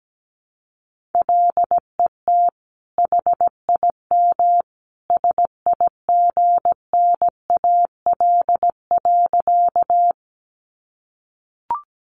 Morse code, heard as a keyed tone, about 17 wpm, 700 Hz. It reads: LET HIM SIGNAL.
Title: Sentences from Top 1400 Words 17wpm